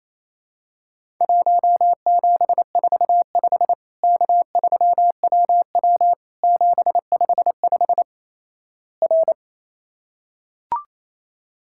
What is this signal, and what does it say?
Morse code, heard as a keyed tone, about 28 wpm, 700 Hz. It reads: 1745 K3WW 755 R
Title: Worked All Europe DX Contest - QTC Exchange - 28wpm